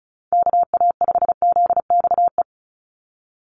Morse code, heard as a keyed tone, about 34 wpm, 700 Hz. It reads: KA5ZXE
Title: USA Call Signs 35wpm